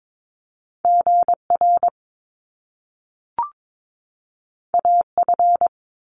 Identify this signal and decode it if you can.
Morse code, heard as a keyed tone, about 22 wpm, 700 Hz. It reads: GR AF
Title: Top Two Letter Combinations in Words 22wpm